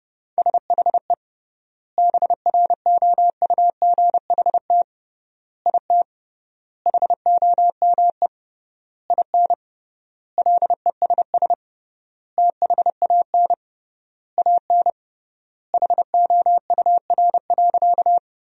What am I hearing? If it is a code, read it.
Morse code, heard as a keyed tone, about 32 wpm, 700 Hz. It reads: SHE BROUGHT IT HOME IN LESS THAN AN HOUR.
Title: 3x Word-Spacing - Sentences from Top 500 Words - 30wpm